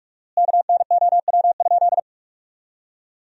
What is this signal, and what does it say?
Morse code, heard as a keyed tone, about 44 wpm, 700 Hz. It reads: KNOW?
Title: 4x Word-Spacing - Sentences from Top 100 Words - 45wpm